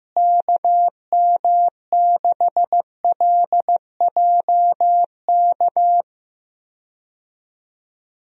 Morse code, heard as a keyed tone, about 15 wpm, 700 Hz. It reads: KM6LJK
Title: USA Call Signs - 6 Characters - Repeated 2x - 15wpm